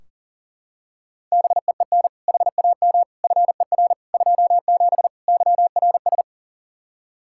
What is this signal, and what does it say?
Morse code, heard as a keyed tone, about 40 wpm, 700 Hz. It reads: BEEN HAM FER 27 YRS
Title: Practice QSO - Exchange 3 - More Info - 40wpm